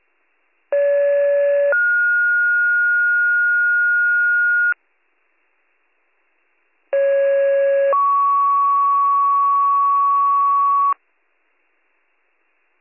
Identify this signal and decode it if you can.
Two-tone page, heard as a one-second tone, then a three-second tone, about 570 then 1430 Hz, about 550 then 1120 Hz. It is heard from 0.7 to 10.9 s.